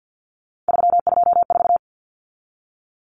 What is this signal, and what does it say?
Morse code, heard as a keyed tone, about 50 wpm, 700 Hz. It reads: 324
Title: Numbers - 3-Digit 50wpm